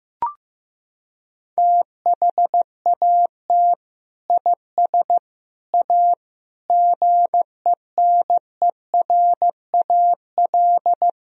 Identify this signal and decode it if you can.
Morse code, heard as a keyed tone, about 15 wpm, 700 Hz. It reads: THAT IS A GENERAL
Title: Sentences from Top 1000 Words - Review Part 3 - 15wpm